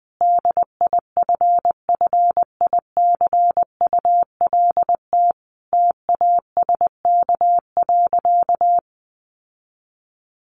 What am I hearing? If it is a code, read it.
Morse code, heard as a keyed tone, about 19 wpm, 700 Hz. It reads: DIFFICULT TASK.
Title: Sentences from Top 1300 Words 20wpm